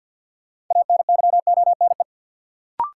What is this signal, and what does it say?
Morse code, heard as a keyed tone, about 53 wpm, 700 Hz. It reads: ANYONE
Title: Sentences from Top 900 Words - New Word Drill - 50wpm